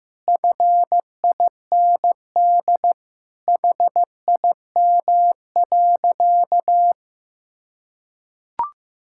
Morse code, heard as a keyed tone, about 16 wpm, 700 Hz. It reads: FIND HIM.
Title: Sentences from Top 1000 Words - Review Part 5 - 15wpm